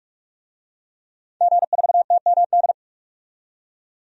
Morse code, heard as a keyed tone, about 43 wpm, 700 Hz. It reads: G4TMD
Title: DXCC-100 Call Signs - 5 Characters - 45wpm